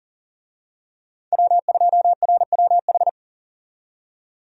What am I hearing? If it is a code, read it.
Morse code, heard as a keyed tone, about 40 wpm, 700 Hz. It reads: W2RWH
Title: USA Call Signs - 5 Characters - 40wpm